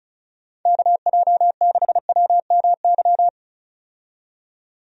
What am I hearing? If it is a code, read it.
Morse code, heard as a keyed tone, about 36 wpm, 700 Hz. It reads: KJ6WMY
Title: USA Call Signs - 6 Characters - 35wpm